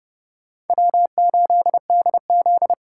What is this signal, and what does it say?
Morse code, heard as a keyed tone, about 39 wpm, 700 Hz. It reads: W8DZ